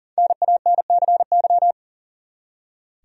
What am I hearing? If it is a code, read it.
Morse code, heard as a keyed tone, about 40 wpm, 700 Hz. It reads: NANCY
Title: USA Common Names - 40wpm